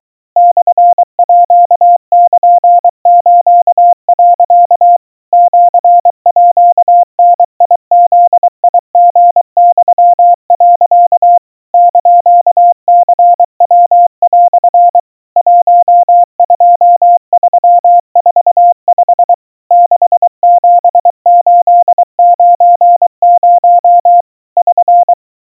Morse code (Japanese Wabun code), heard as a keyed tone, about 23 wpm, 700 Hz. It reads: モールス、 シーダブリユ、 <KN>ニヤ) 12345 67890 <SN>
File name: kawa / Morse sample / aiueo 80 cpm 700 Hz Crispy